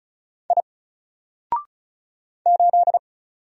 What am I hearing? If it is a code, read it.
Morse code, heard as a keyed tone, about 35 wpm, 700 Hz. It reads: I 8
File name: Single-Letter-Number-35wpm